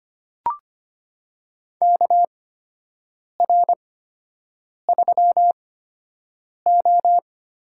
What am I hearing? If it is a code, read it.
Morse code, heard as a keyed tone, about 25 wpm, spaced as for 7 wpm, 700 Hz. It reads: KR3O
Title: USA Call Signs - 6 Characters - 8x Character Spacing - 25wpm